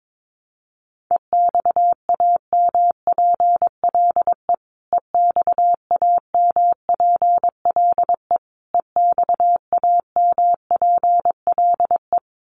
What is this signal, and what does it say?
Morse code, heard as a keyed tone, about 22 wpm, 700 Hz. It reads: EXAMPLE EXAMPLE EXAMPLE
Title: Top 500 Words Repeated 22wpm